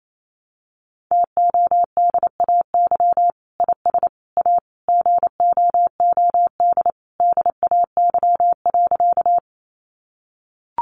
Morse code, heard as a keyed tone, about 28 wpm, 700 Hz. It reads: TODAY IS A GOOD DAY.
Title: Sentences from Top 600 Words 28wpm